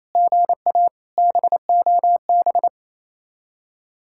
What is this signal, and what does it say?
Morse code, heard as a keyed tone, about 37 wpm, 700 Hz. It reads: GA BOB